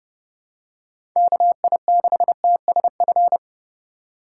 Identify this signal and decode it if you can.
Morse code, heard as a keyed tone, about 30 wpm, 700 Hz. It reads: KI6TSF